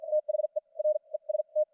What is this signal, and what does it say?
Morse code, heard as a keyed tone, about 54 wpm, spaced as for 39 wpm, 610 Hz, with radio noise. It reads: THE REST
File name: archived / CRSnet